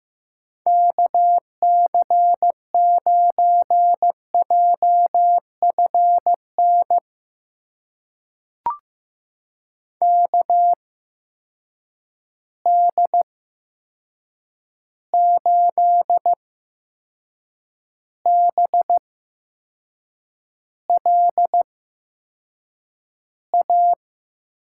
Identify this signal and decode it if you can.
Morse code, heard as a keyed tone, about 15 wpm, spaced as for 6 wpm, 700 Hz. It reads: KC9JFN KD8BLA
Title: USA Call Signs - 6 Characters - 8x Character Spacing - 15wpm